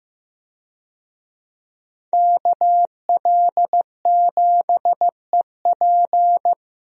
Morse code, heard as a keyed tone, about 15 wpm, 700 Hz. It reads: KL7EP
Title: DXCC-100 Call Signs - 5 Characters - 15wpm